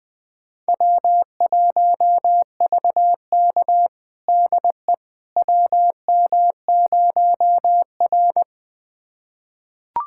Morse code, heard as a keyed tone, about 21 wpm, 700 Hz. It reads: W1VK DE WM0R
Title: USA Callsign DE Callsign - 4 Characters - 20wpm